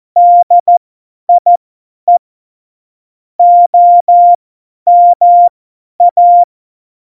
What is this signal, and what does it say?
Morse code, heard as a keyed tone, about 14 wpm, spaced as for 10 wpm, 700 Hz. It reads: DIE OMA
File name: Morsekurs / Witze / Witz 17. Oktober 17, Tempo 50